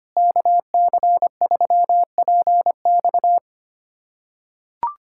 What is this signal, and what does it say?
Morse code, heard as a keyed tone, about 25 wpm, 700 Hz. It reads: KC3PX
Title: USA Callsign DE Callsign - 5 Characters - 25wpm